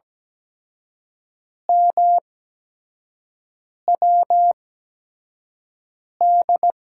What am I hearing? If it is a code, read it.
Morse code, heard as a keyed tone, about 17 wpm, spaced as for 5 wpm, 700 Hz. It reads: MWD
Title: USA Call Signs - 8x Character Spacing - 17wpm